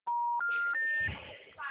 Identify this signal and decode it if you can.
Special information tone, with noise.
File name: out- -2021- - - .68228